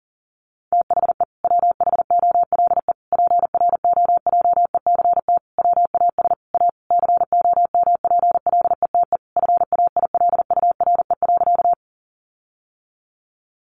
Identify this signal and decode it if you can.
Morse code, heard as a keyed tone, about 37 wpm, 700 Hz. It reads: THE WHOLE PROJECT WAS A COMPLETE FAILURE.
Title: Sentences from Top 1400 Words 40wpm